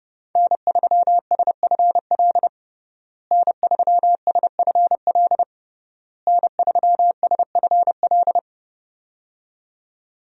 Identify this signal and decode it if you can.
Morse code, heard as a keyed tone, about 28 wpm, 700 Hz. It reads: N3SFL N3SFL N3SFL
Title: USA Call Signs - 5 Characters - Repeated 3x - 30wpm